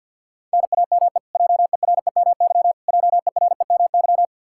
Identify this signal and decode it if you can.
Morse code, heard as a keyed tone, about 51 wpm, 700 Hz. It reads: NAME JEREMY JEREMY
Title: Practice QSO - Exchange 1 - Beginning - 50wpm